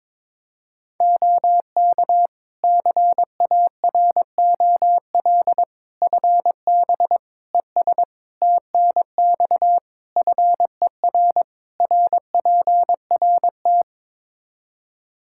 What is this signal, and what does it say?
Morse code, heard as a keyed tone, about 22 wpm, 700 Hz. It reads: OK CAROL FB ES TNX FER RPRT
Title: K1USN SST QSO Practice 22wpm